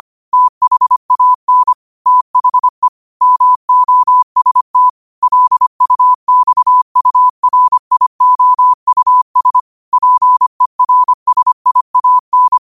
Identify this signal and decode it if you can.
Morse code, heard as a keyed tone, about 25 wpm, 1000 Hz. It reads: THAN THE MOST LUXURIOUS PERSIAN
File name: qotdfiles / 25wpm